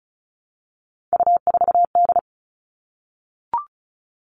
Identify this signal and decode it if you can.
Morse code, heard as a keyed tone, about 34 wpm, 700 Hz. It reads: U4D
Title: All Cut Numbers - 3-Digit 35wpm